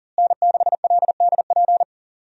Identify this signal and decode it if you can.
Morse code, heard as a keyed tone, about 39 wpm, 700 Hz. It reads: N6LDP
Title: USA Call Signs - 5 Characters - Repeated 2x - 40wpm